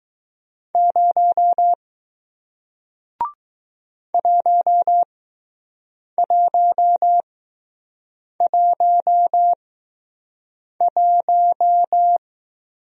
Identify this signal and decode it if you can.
Morse code, heard as a keyed tone, about 20 wpm, spaced as for 6 wpm, 700 Hz. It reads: 0 1111